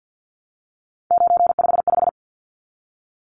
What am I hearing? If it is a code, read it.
Morse code, heard as a keyed tone, about 50 wpm, 700 Hz. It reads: 955
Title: Numbers - 3-Digit 50wpm